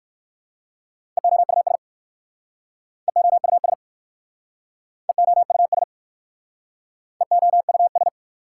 Morse code, heard as a keyed tone, about 58 wpm, 700 Hz. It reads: EOUS EOUS EOUS EOUS